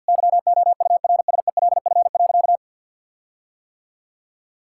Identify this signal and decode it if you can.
Morse code, heard as a keyed tone, about 50 wpm, 700 Hz. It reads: YOURSELF.
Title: Sentences from Top 1600 Words 50wpm